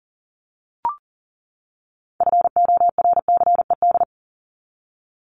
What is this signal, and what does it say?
Morse code, heard as a keyed tone, about 40 wpm, 700 Hz. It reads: FORCED
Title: ED Suffix - Top 100 Words with Base Words 40wpm